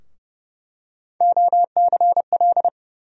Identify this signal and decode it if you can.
Morse code, heard as a keyed tone, about 32 wpm, 700 Hz. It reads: OCL
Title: USA Call Sign Postfixes - 3 Letters 30wpm